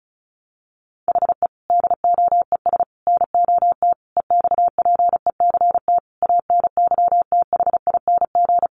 Morse code, heard as a keyed tone, about 34 wpm, 700 Hz. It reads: HE DOES NOT EXPECT ANYTHING
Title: Sentences from Top 1100 Words 35wpm